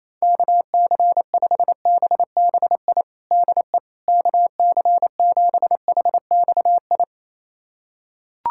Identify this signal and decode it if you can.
Morse code, heard as a keyed tone, about 28 wpm, 700 Hz. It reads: KC5BBI DE KC7HXI
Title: USA Callsign DE Callsign - 6 Characters - 28wpm